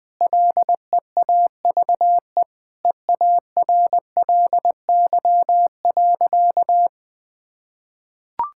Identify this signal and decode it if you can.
Morse code, heard as a keyed tone, about 20 wpm, 700 Hz. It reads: LEAVE EARLY.
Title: Sentences with 5-Word Phrases - 20wpm